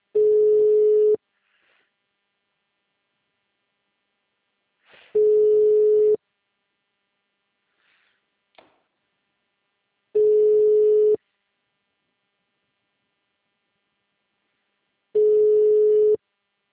Ringback tone (Europe, 425 Hz).